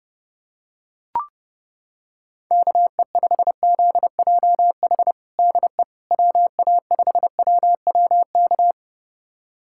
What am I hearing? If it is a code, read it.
Morse code, heard as a keyed tone, about 30 wpm, 700 Hz. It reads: KE5ZJH DE WA5WWK
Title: USA Callsign DE Callsign - 6 Characters - 30wpm